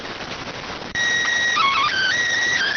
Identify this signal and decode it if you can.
SSTV (slow-scan television), PD 290.